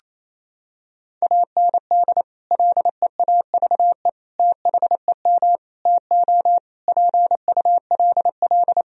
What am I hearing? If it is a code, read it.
Morse code, heard as a keyed tone, about 27 wpm, 700 Hz. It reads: AND LEAVE THEM TO PULL
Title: Sentences from Top 1000 Words - Review Part 2 - 28wpm